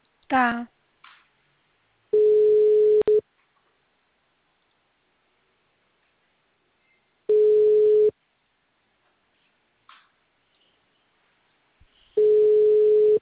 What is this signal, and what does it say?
Ringback tone (Europe, 425 Hz), with speech.